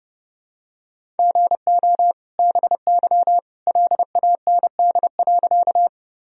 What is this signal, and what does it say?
Morse code, heard as a keyed tone, about 30 wpm, 700 Hz. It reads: GO BY LAND.
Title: Sentences from Top 1000 Words - Review Part 1 - 30wpm